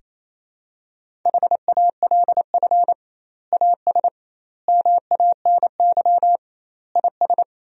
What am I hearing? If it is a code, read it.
Morse code, heard as a keyed tone, about 28 wpm, 700 Hz. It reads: HALF AS MANY IS
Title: 2x Word-Spacing - Sentences from Top 400 Words - 28wpm